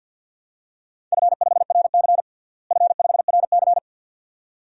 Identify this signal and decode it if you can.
Morse code, heard as a keyed tone, about 52 wpm, 700 Hz. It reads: F5RC F5RC
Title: DXCC-100 Call Signs - 4 Characters - Repeated 2x - 50wpm